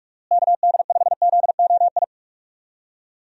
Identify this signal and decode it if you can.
Morse code, heard as a keyed tone, about 45 wpm, 700 Hz. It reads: KD5ZOI
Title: USA Call Signs - 6 Characters - Repeated 3x - 45wpm